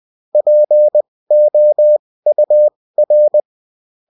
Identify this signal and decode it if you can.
Morse code, heard as a keyed tone, about 20 wpm, spaced as for 16 wpm, 600 Hz. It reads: POUR